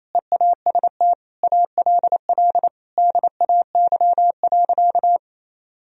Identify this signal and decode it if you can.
Morse code, heard as a keyed tone, about 28 wpm, 700 Hz. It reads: EAST ALL DAY.
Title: Sentences from Top 1000 Words - Review Part 3 - 28wpm